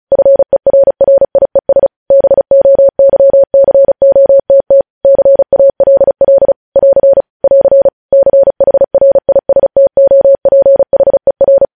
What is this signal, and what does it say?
Morse code, heard as a keyed tone, about 36 wpm, 550 Hz. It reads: FERRIES BOYCOTT CALL <AR> <AR> CHRISTOPHER